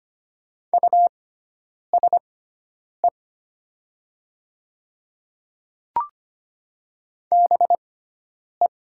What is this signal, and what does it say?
Morse code, heard as a keyed tone, about 25 wpm, spaced as for 9 wpm, 700 Hz. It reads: USE BE
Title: 6x Character Spacing - Top 100 Words - 25wpm